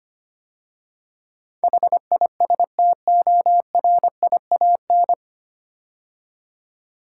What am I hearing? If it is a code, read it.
Morse code, heard as a keyed tone, about 25 wpm, 700 Hz. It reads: HISTORIAN
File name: practice / 25wpm / 9 Letter Words - First 3 Letters Given - 25wpm